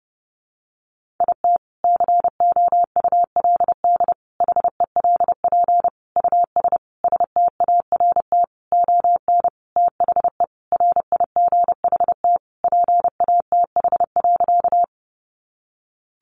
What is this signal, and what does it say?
Morse code, heard as a keyed tone, about 28 wpm, 700 Hz. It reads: IT COULD HELP US START ON THE RIGHT PATH.
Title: Sentences from Top 1400 Words 30wpm